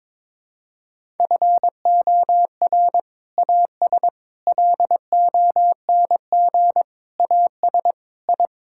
Morse code, heard as a keyed tone, about 21 wpm, 700 Hz. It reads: FOR AS LONG AS I
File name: practice / 22wpm / Sets of 8 Words - 22wpm